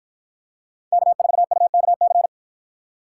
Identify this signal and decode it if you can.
Morse code, heard as a keyed tone, about 53 wpm, 700 Hz. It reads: K4UKC